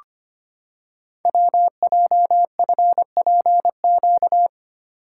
Morse code, heard as a keyed tone, about 25 wpm, 700 Hz. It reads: WJFPQ
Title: Code Groups - 5 Letters - 25wpm